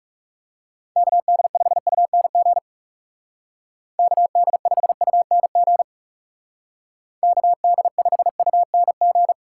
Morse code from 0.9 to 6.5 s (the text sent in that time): KD5UNG KD5UNG